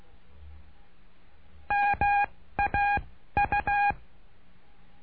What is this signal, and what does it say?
Morse code, heard as a keyed tone, about 16 wpm, spaced as for 13 wpm, 810 Hz. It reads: MAU
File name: cursos / CURSO DE TELEGRAFIA / Telegrafia - Trilha 09